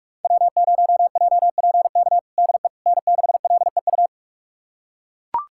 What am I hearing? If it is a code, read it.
Morse code, heard as a keyed tone, about 42 wpm, 700 Hz. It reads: W0JPK DE N6LEU
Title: USA Callsign DE Callsign - 5 Characters - 45wpm